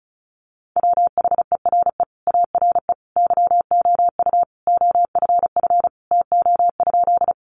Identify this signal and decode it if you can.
Morse code, heard as a keyed tone, about 34 wpm, 700 Hz. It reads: WHERE ARE YOU OFF TO?
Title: Sentences from Top 200 Words 35wpm